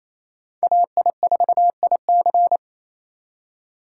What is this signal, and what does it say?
Morse code, heard as a keyed tone, about 28 wpm, 700 Hz. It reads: AI4IC
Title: USA Call Signs - 5 Characters - 2x Character Spacing - 28wpm